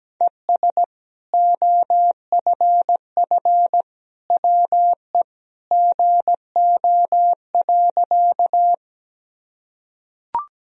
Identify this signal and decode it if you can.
Morse code, heard as a keyed tone, about 17 wpm, 700 Hz. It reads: ES OFF WE GO.
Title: Sentences from Top 200 Words and Abbreviations 17wpm